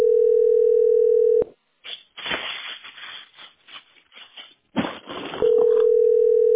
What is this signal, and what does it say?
Ringback tone (North America), 440 + 480 Hz.
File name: 4b -9122-4cd3-bad4